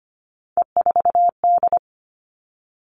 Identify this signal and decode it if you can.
Morse code, heard as a keyed tone, about 25 wpm, 700 Hz. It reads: E4D
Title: All Cut Numbers - 3-Digit 25wpm